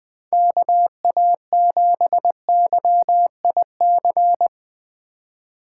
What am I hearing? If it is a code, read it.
Morse code, heard as a keyed tone, about 22 wpm, 700 Hz. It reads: KA7YIC
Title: Speed Racing - USA 2x3 Call Signs 20wpm